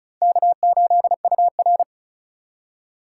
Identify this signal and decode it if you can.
Morse code, heard as a keyed tone, about 36 wpm, 700 Hz. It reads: K8UR
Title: USA Call Signs - 4x Character Spacing - 35wpm